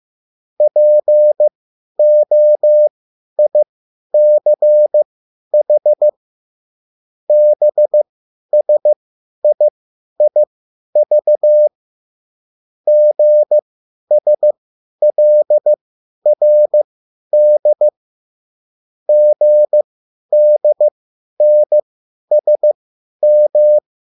Morse code, heard as a keyed tone, about 15 wpm, spaced as for 10 wpm, 600 Hz. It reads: POICH BSIIV GSLRD GDNSM